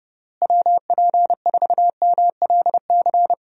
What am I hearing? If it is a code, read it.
Morse code, heard as a keyed tone, about 30 wpm, 700 Hz. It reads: WP4MLC
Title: USA Call Signs - 6 Characters - Repeated 2x - 30wpm